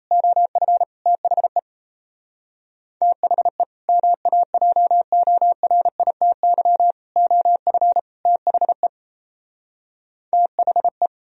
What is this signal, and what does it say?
Morse code, heard as a keyed tone, about 37 wpm, 700 Hz. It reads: OF THE THE MAJORITY OF THE THE